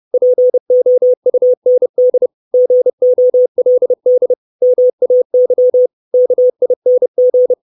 Morse code, heard as a keyed tone, about 30 wpm, 500 Hz. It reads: POUND GOLD MAY KING
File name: 632 Words 30WPM